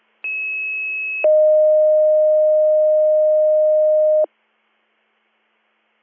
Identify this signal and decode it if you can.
Two-tone page, heard as a one-second tone, then a three-second tone, about 2600 then 620 Hz. It is heard from 0.2 to 4.2 s.